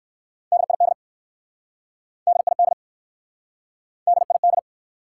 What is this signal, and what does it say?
Morse code, heard as a keyed tone, about 60 wpm, 700 Hz. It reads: DID DID DID